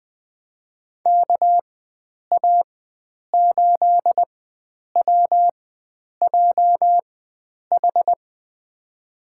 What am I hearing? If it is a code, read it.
Morse code, heard as a keyed tone, about 20 wpm, spaced as for 9 wpm, 700 Hz. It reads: KA8WJH
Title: USA Call Signs - 4x Character Spacing - 20wpm